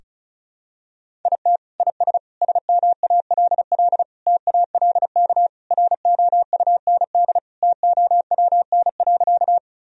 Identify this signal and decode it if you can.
Morse code, heard as a keyed tone, about 36 wpm, 700 Hz. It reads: IT IS SMALL TALK ROUND TOWN.
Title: Sentences from Top 1000 Words - Review Part 1 - 35wpm